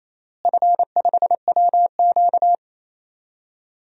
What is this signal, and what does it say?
Morse code, heard as a keyed tone, about 28 wpm, 700 Hz. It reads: F5WQ
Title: DXCC-100 Call Signs - 4 Characters - Repeated 3x - 28wpm